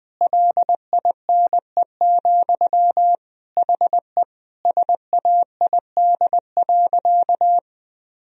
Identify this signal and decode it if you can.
Morse code, heard as a keyed tone, about 21 wpm, 700 Hz. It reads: LINE, HE SAID.
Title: Sentences from Top 1000 Words - Review Part 1 - 20wpm